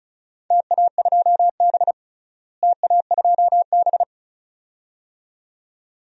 Morse code, heard as a keyed tone, about 36 wpm, 700 Hz. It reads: TA2B TA2B
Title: DXCC-100 Call Signs - 4 Characters - Repeated 2x - 35wpm